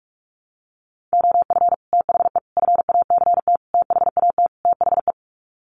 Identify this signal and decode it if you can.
Morse code, heard as a keyed tone, about 44 wpm, 700 Hz. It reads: OF THE FACT THAT THE
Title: Sets of 5 Words Encore2 45wpm